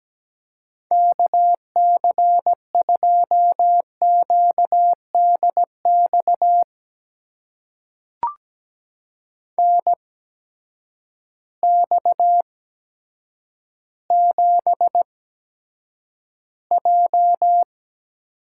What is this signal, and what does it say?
Morse code, heard as a keyed tone, about 17 wpm, spaced as for 8 wpm, 700 Hz. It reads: KC2QDX NX7J
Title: USA Call Signs - 8x Character Spacing - 17wpm